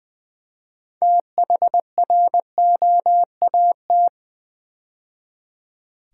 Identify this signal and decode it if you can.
Morse code, heard as a keyed tone, about 20 wpm, 700 Hz. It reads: THROAT
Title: 6 Letter Words - First 3 Letters Given - 20wpm